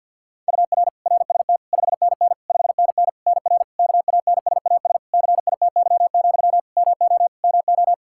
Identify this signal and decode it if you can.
Morse code, heard as a keyed tone, about 52 wpm, 700 Hz. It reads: UR RST 5NN 5NN NR KANSAS CITY, MO MO
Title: Practice QSO - Exchange 1 - Beginning - 50wpm